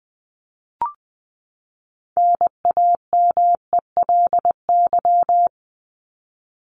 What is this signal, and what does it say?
Morse code, heard as a keyed tone, about 20 wpm, 700 Hz. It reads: NAMELY
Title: LY Suffix - Top 100 Words with Base Words 20wpm